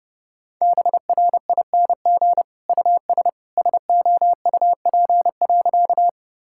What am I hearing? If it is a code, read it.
Morse code, heard as a keyed tone, about 30 wpm, 700 Hz. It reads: BRING US SOUP.
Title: Sentences from Top 1000 Words - Review Part 6 - 30wpm